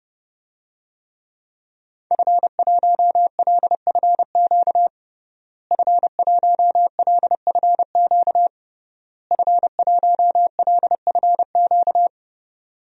Morse code, heard as a keyed tone, about 32 wpm, 700 Hz. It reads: F1LFQ F1LFQ F1LFQ
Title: DXCC-100 Call Signs - 5 Characters - Repeated 3x - 30wpm